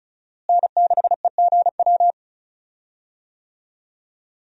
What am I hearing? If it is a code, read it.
Morse code, heard as a keyed tone, about 36 wpm, 700 Hz. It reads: N6EGW
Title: USA Call Signs - 5 Characters - Repeated 3x - 35wpm